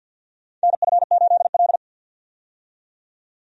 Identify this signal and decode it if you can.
Morse code, heard as a keyed tone, about 52 wpm, 700 Hz. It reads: NF8L
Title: USA Call Signs - 4 Characters - 50wpm